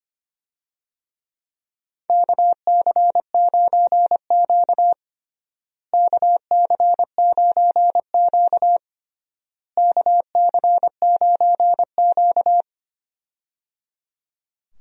Morse code, heard as a keyed tone, about 25 wpm, 700 Hz. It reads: KC9Q KC9Q KC9Q
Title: USA Call Signs - 4 Characters - Repeated 3x - 25wpm